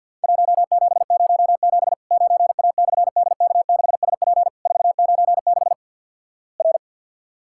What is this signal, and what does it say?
Morse code, heard as a keyed tone, about 51 wpm, 700 Hz. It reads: 1707 9A/DK6SP 496 R
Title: Worked All Europe DX Contest - QTC Exchange with Repeats - 50wpm